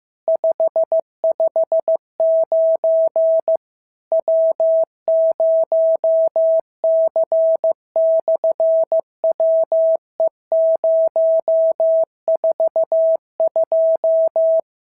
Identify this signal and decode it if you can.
Morse code, heard as a keyed tone, about 15 wpm, 650 Hz. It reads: 559 W0C/WE042
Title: SOTA QSO Practice 15wpm